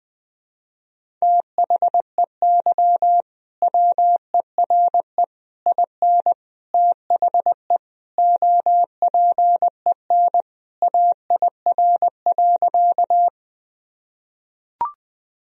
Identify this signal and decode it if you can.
Morse code, heard as a keyed tone, about 21 wpm, 700 Hz. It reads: THEY WERE IN THE OPEN AIR.
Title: Sentences from Top 1000 Words - Review Part 1 - 20wpm